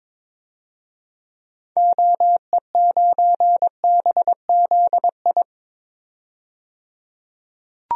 Morse code, heard as a keyed tone, about 21 wpm, 700 Hz. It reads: OE9BZI
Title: DXCC-100 Call Signs - 6 Characters - 22wpm